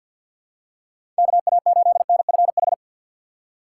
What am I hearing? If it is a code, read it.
Morse code, heard as a keyed tone, about 48 wpm, 700 Hz. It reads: KA8NFH
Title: USA Call Signs - 6 Characters - 4x Character Spacing - 50wpm